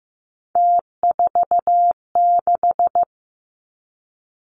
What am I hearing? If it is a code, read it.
Morse code, heard as a keyed tone, about 15 wpm, 700 Hz. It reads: T46
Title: Three Challenging Letter-Number Combinations 15wpm